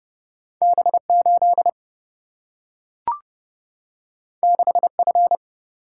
Morse code, heard as a keyed tone, about 30 wpm, 700 Hz. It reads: B8 6F
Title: Single-Letter Single-Number 30wpm